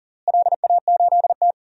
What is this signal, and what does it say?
Morse code, heard as a keyed tone, about 39 wpm, 700 Hz. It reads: LA8T
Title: DXCC-100 Call Signs - 4 Characters - Repeated 3x - 40wpm